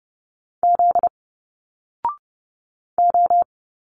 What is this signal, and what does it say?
Morse code, heard as a keyed tone, about 30 wpm, 700 Hz. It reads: Z O
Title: Single Letters 30wpm